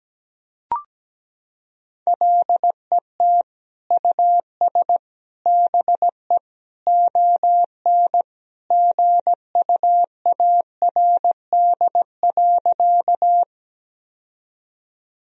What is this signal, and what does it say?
Morse code, heard as a keyed tone, about 17 wpm, 700 Hz. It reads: LET US BE ON GUARD.
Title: Sentences from Top 1000 Words - Review Part 4 - 17wpm